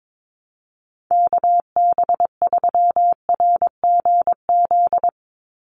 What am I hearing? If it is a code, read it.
Morse code, heard as a keyed tone, about 22 wpm, 700 Hz. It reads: KB3RGZ
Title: USA Call Signs 22wpm